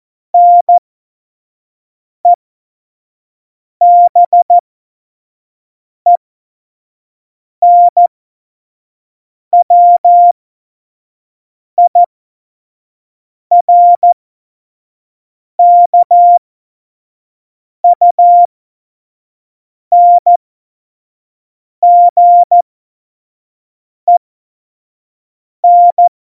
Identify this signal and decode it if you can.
Morse code, heard as a keyed tone, about 14 wpm, spaced as for 5 wpm, 700 Hz. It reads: NEBENWIRKUNGEN
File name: Morsekurs / Witze / Witz 20 March 2018, 25 F70